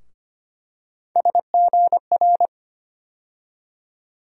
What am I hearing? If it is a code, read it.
Morse code, heard as a keyed tone, about 25 wpm, 700 Hz. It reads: SGR
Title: USA Call Sign Postfixes - 3 Letters 25wpm